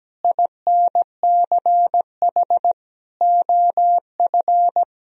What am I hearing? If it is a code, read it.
Morse code, heard as a keyed tone, about 17 wpm, 700 Hz. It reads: INCH OF
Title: Sentences from Top 1000 Words - Review Part 3 - 17wpm